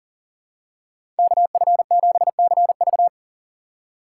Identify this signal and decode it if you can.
Morse code, heard as a keyed tone, about 40 wpm, 700 Hz. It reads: KF7CV
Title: USA Call Signs - 3x Character Spacing - 40wpm